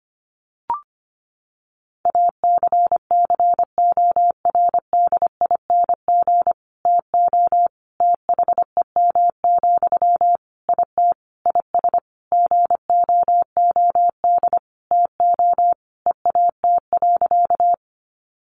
Morse code, heard as a keyed tone, about 25 wpm, 700 Hz. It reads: ACCORDING TO THEM, IT IS GOOD TO EAT.
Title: Sentences from Top 800 Words 25wpm